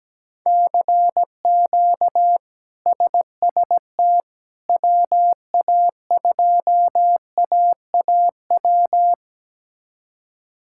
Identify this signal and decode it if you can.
Morse code, heard as a keyed tone, about 17 wpm, 700 Hz. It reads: CQ SST WA2AAW
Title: K1USN SST QSO Practice - 17wpm